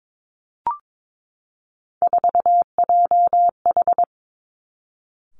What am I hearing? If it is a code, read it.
Morse code, heard as a keyed tone, about 22 wpm, 700 Hz. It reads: 4JH
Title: Three Challenging Letter-Number Combinations 22wpm